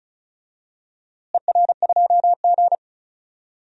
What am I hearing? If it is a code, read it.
Morse code, heard as a keyed tone, about 34 wpm, 690 Hz. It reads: ER2G
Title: DXCC-100 Call Signs - 4 Characters - Repeated 2x - 35wpm